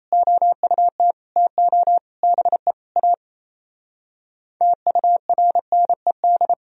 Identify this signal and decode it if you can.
Morse code, heard as a keyed tone, about 29 wpm, 700 Hz. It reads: OUT TO BE A TURNED
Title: Speed-Racing - Sets of 5 Words 28wpm